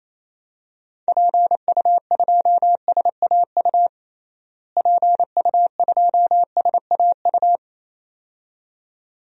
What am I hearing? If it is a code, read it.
Morse code, heard as a keyed tone, about 28 wpm, 700 Hz. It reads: PU2SAU PU2SAU
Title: DXCC-100 Call Signs - 6 Characters - Repeated 2x - 28wpm